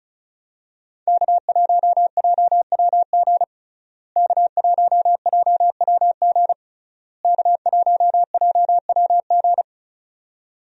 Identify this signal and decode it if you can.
Morse code, heard as a keyed tone, about 36 wpm, 700 Hz. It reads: K1JWG K1JWG K1JWG
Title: USA Call Signs - 5 Characters - Repeated 3x - 35wpm